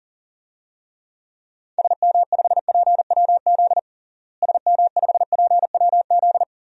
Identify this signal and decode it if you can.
Morse code, heard as a keyed tone, about 40 wpm, 700 Hz. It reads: SM5PWZ SM5PWZ
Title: DXCC-100 Call Signs - 6 Characters - Repeated 3x - 40wpm